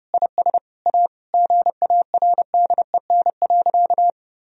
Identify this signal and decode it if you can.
Morse code, heard as a keyed tone, about 30 wpm, 700 Hz. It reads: IS A GARDEN.
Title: Sentences from Top 1000 Words - Review Part 4 - 30wpm